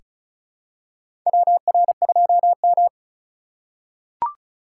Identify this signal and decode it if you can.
Morse code, heard as a keyed tone, about 34 wpm, 700 Hz. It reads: WR2M
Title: USA Call Signs - 4 Characters - 6x Character Spacing - 35wpm